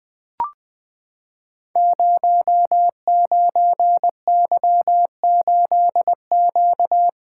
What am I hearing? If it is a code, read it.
Morse code, heard as a keyed tone, about 20 wpm, 700 Hz. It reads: 09Y8Q
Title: Code Groups - 5 Letter:Numbers - 20wpm